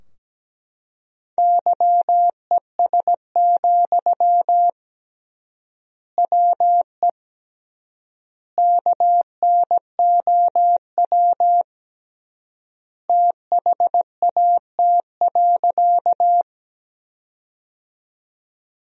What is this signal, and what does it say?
Morse code, heard as a keyed tone, about 17 wpm, 700 Hz. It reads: YES, WE KNOW THAT.
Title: 3x Word-Spacing - Sentences from Top 500 Words - 17wpm